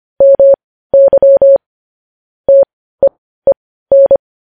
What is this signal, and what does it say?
Morse code, heard as a keyed tone, about 25 wpm, spaced as for 15 wpm, 550 Hz. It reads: MY TEEN
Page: news-headlines-in-morse-code-at-15-wpm